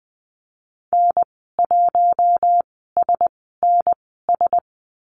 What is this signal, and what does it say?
Morse code, heard as a keyed tone, about 20 wpm, spaced as for 15 wpm, 700 Hz. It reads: N1SNS
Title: 042 Lesson 8g - Number 1 - Call Signs 20wpm